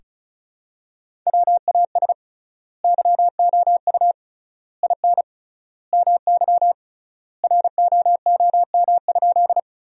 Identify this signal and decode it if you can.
Morse code, heard as a keyed tone, about 36 wpm, 700 Hz. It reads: WAS YOU IN MY ROOM?
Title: 3x Word-Spacing - Sentences from Top 300 Words - 35wpm